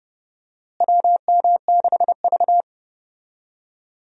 Morse code, heard as a keyed tone, about 30 wpm, 700 Hz. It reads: WM6V
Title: USA Call Signs - 4 Characters - Repeated 2x - 30wpm